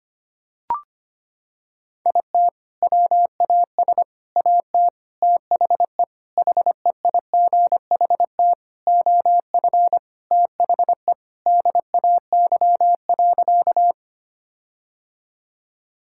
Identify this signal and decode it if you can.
Morse code, heard as a keyed tone, about 25 wpm, 700 Hz. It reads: IT WAS AT THE HEIGHT OF THE DAY.
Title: Sentences from Top 1600 Words 25wpm